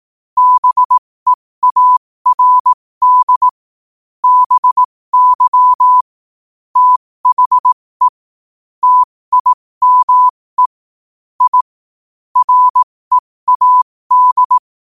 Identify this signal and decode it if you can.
Morse code, heard as a keyed tone, about 18 wpm, spaced as for 15 wpm, 1000 Hz. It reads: BEARD BY THE TIME I READ
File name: qotdfiles / 15wpm